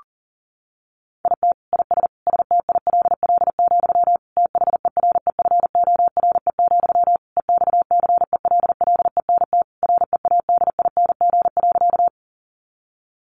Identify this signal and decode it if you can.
Morse code, heard as a keyed tone, about 40 wpm, 700 Hz. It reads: IT IS STILL, THEREFORE, EXCELLENT READING.
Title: Sentences from Top 1000 Words 40wpm